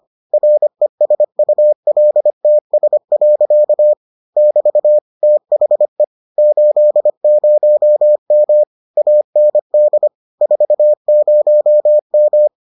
Morse code, heard as a keyed tone, about 25 wpm, 600 Hz. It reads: RESULTS. <BT> THE 80M AND 40M